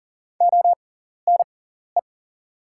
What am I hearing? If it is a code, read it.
Morse code, heard as a keyed tone, about 40 wpm, spaced as for 14 wpm, 690 Hz. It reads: ONE
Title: 6x Character Spacing - Top 300 Words - 40wpm